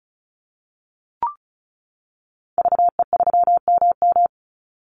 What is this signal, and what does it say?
Morse code, heard as a keyed tone, about 34 wpm, 700 Hz. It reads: VE3MM
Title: Call Signs - 35wpm